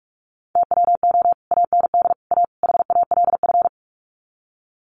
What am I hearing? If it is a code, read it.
Morse code, heard as a keyed tone, about 44 wpm, 700 Hz. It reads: TWO AND A HALF
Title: Sets of 4 Words 45wpm